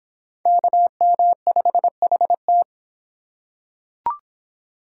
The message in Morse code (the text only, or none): KM5HT